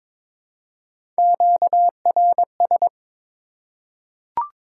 Morse code, heard as a keyed tone, about 22 wpm, 700 Hz. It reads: QRS
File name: Top 90 QSO Elements - 4x Character Spacing - 22wpm